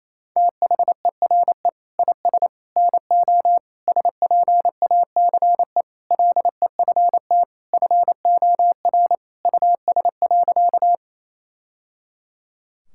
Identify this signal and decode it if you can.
Morse code, heard as a keyed tone, about 28 wpm, 700 Hz. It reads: THERE IS NO SPACE LEFT FOR US.
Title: Sentences from Top 1000 Words - Review Part 5 - 28wpm